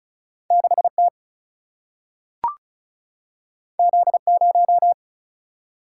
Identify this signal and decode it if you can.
Morse code, heard as a keyed tone, about 36 wpm, 700 Hz. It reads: 6 T Z 0
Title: Single-Letter Single-Number 35wpm